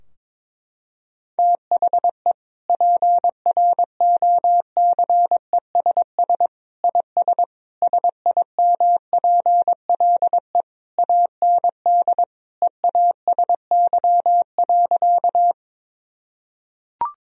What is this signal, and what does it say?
Morse code, heard as a keyed tone, about 21 wpm, 700 Hz. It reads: THE PROCESS IS SIMPLE AND EASY.
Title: Sentences from Top 1000 Words - Review Part 3 - 22wpm